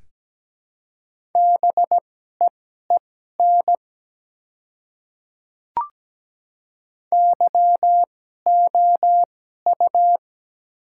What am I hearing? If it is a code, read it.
Morse code, heard as a keyed tone, about 17 wpm, spaced as for 12 wpm, 700 Hz. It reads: BEEN YOU
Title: 2x Character Spacing - Top 200 Words - 17wpm